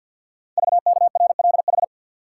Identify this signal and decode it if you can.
Morse code, heard as a keyed tone, about 50 wpm, 700 Hz. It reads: VKRLH
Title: Code Groups - 5 Letters - 50wpm